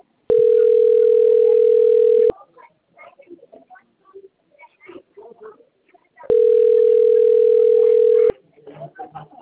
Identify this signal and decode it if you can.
Ringback tone (North America), 440 + 480 Hz, with speech.
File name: out- - - - - .51179